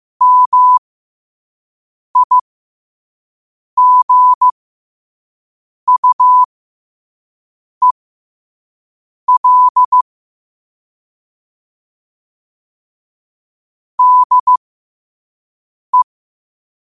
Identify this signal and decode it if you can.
Morse code, heard as a keyed tone, about 15 wpm, spaced as for 5 wpm, 1000 Hz. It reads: MIGUEL DE
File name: qotdfiles / 5wpm